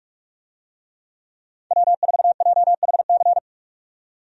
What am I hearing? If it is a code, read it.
Morse code, heard as a keyed tone, about 45 wpm, 700 Hz. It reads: W4JHC
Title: USA Call Signs - 5 Characters - 45wpm